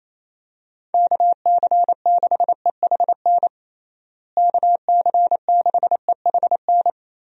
Morse code, heard as a keyed tone, about 28 wpm, 700 Hz. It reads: KC6EHN KC6EHN
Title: USA Call Signs - 6 Characters - Repeated 3x - 28wpm